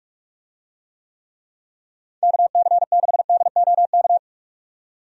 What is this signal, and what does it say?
Morse code, heard as a keyed tone, about 44 wpm, 700 Hz. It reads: KC6DOK
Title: USA Call Signs - 6 Characters - 45wpm